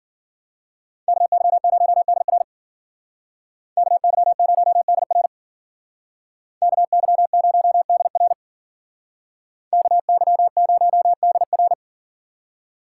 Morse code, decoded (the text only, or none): KY0DR KY0DR KY0DR KY0DR